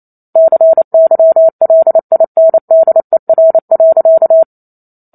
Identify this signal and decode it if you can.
Morse code, heard as a keyed tone, about 28 wpm, 650 Hz. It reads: CYLINDER.
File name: Chapter 2 - 28 WPM